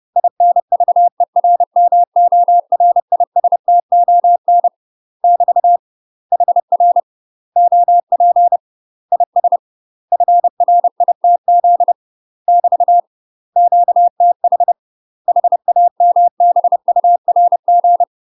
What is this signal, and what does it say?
Morse code, heard as a keyed tone, about 30 wpm, 700 Hz. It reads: INVERMORISTON <BT> HR OP IS FRITZ <BT> QTH HAMBURG
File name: UK MASTER QSOs TRAINING / 30WPM / UK MASTER QSOs 0046